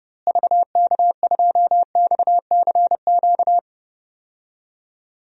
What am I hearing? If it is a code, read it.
Morse code, heard as a keyed tone, about 30 wpm, 700 Hz. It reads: VK2XCQ
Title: DXCC-100 Call Signs - 6 Characters - Repeated 3x - 30wpm